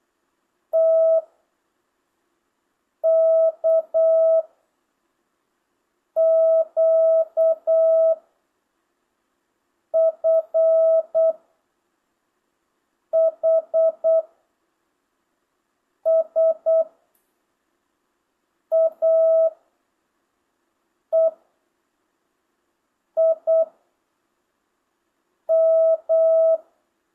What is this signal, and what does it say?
Morse code, heard as a keyed tone, about 8 wpm, 650 Hz. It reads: T K Q F H S A E I M
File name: jh8cbh / 2025cw